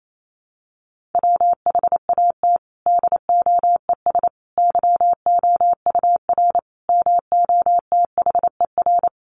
Morse code, heard as a keyed tone, about 28 wpm, 700 Hz. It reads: WHAT DOES YOUR MOTHER
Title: Sentences from Top 1400 Words 28wpm